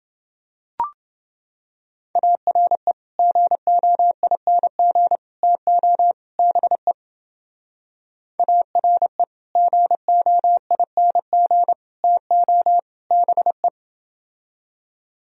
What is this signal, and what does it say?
Morse code, heard as a keyed tone, about 30 wpm, 700 Hz. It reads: ARE GOING TO BE ARE GOING TO BE